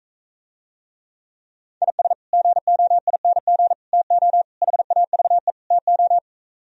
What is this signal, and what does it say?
Morse code, heard as a keyed tone, about 45 wpm, 700 Hz. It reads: IS GOING TO HAVE TO